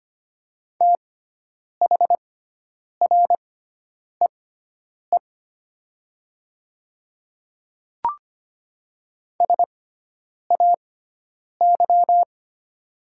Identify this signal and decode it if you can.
Morse code, heard as a keyed tone, about 25 wpm, spaced as for 9 wpm, 700 Hz. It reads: THREE SAY
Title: 6x Character Spacing - Top 300 Words - 25wpm